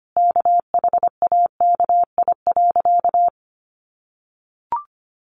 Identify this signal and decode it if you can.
Morse code, heard as a keyed tone, about 25 wpm, 700 Hz. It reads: KHAKI.
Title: Additional Easy Phrases 25wpm